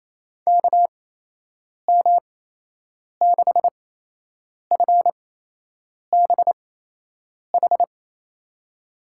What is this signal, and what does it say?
Morse code, heard as a keyed tone, about 28 wpm, spaced as for 8 wpm, 700 Hz. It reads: KM6FBH
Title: USA Call Signs - 6 Characters - 8x Character Spacing - 28wpm